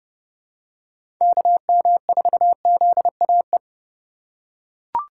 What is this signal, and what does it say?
Morse code, heard as a keyed tone, about 30 wpm, 700 Hz. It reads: KM4ZAE
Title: USA Call Signs - 2x Character Spacing - 30wpm